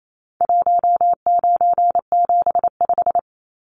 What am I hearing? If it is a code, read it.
Morse code, heard as a keyed tone, about 28 wpm, 700 Hz. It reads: 1975
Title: Numbers - 4-Digit 28wpm